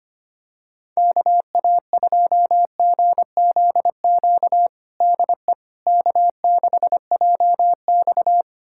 Morse code, heard as a keyed tone, about 25 wpm, 700 Hz. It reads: KA2GZQ DE K6JX
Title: USA Callsign DE Callsign 25wpm